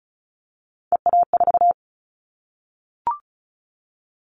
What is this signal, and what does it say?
Morse code, heard as a keyed tone, about 34 wpm, 700 Hz. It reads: EA4
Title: Callsign Prefixes for Top 25 DXCC - 35wpm